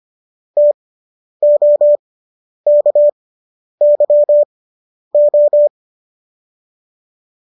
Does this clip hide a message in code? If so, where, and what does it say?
Yes: Morse code from 0.6 to 5.7 s, heard as a keyed tone, about 25 wpm, spaced as for 10 wpm, 590 Hz. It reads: TOKYO